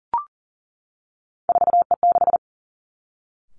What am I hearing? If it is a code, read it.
Morse code, heard as a keyed tone, about 38 wpm, 700 Hz. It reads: 4E6
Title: Three Challenging Letter-Number Combinations 40wpm